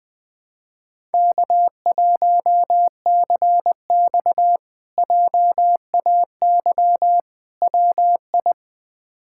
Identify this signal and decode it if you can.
Morse code, heard as a keyed tone, about 20 wpm, 700 Hz. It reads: K1CX JAY WI
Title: K1USN SST QSO Practice - 20wpm